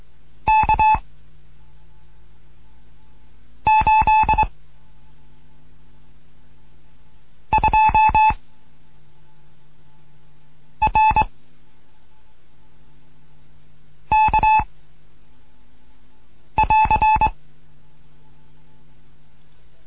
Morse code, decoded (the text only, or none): K 8 2 R K <AR>